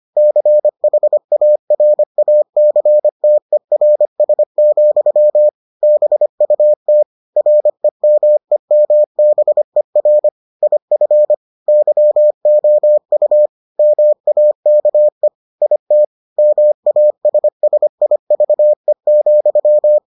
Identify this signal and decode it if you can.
Morse code, heard as a keyed tone, about 25 wpm, 600 Hz. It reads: CHARACTERS, BUT REMEMBER IF YOU MAKE IT MASSIVE,